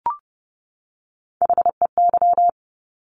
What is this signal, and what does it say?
Morse code, heard as a keyed tone, about 30 wpm, 700 Hz. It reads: HEY
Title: Top Three Letter Combinations in Words 30wpm